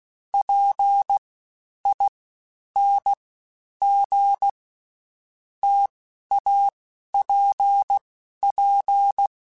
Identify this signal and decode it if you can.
Morse code, heard as a keyed tone, about 16 wpm, 800 Hz. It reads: P I N G T A P P
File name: Difficult Combos / Difficult Combos-mp3s / T AOEI @15